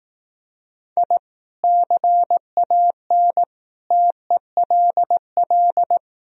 Morse code, heard as a keyed tone, about 18 wpm, 700 Hz. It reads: I CAN TELL
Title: Speed-Racing - Sets of 5 Words 15wpm